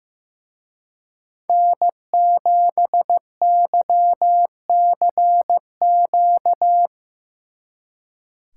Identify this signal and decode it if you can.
Morse code, heard as a keyed tone, about 17 wpm, 700 Hz. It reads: N7YCQ
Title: Speed Racing - USA 1x3 Call Signs 15wpm